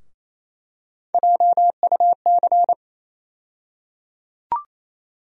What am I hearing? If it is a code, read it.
Morse code, heard as a keyed tone, about 28 wpm, 700 Hz. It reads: JUC
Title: USA Call Sign Postfixes - 3 Letters 28wpm